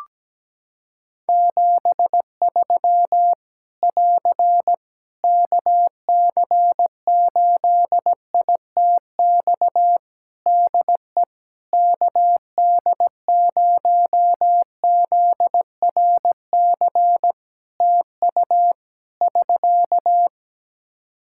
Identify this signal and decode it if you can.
Morse code, heard as a keyed tone, about 17 wpm, 700 Hz. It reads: 73 <AR> KC8ITX DE KD0ZRC TU <SK>
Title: Practice QSO - Exchange 4 - Ending - 17wpm